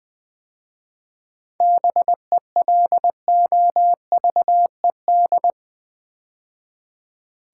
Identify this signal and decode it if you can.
Morse code, heard as a keyed tone, about 21 wpm, 700 Hz. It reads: BELOVED
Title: 7 Letter Words - First 3 Letters Given - 20wpm